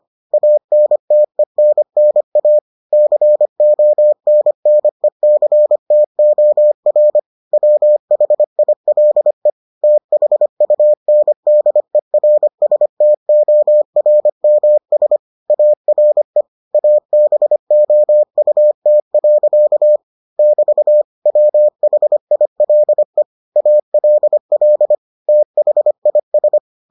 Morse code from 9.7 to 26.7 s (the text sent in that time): THUNDERSTORMS ARE ABOUT. <BT> WHILE ALL THIS